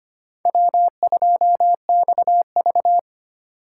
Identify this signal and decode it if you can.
Morse code, heard as a keyed tone, about 25 wpm, 700 Hz. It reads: W2XV
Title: USA Call Signs - 4 Characters - 2x Character Spacing - 25wpm